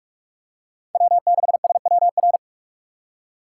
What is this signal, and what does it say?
Morse code, heard as a keyed tone, about 44 wpm, 700 Hz. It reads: W6SWR
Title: USA Call Signs - 4x Character Spacing - 45wpm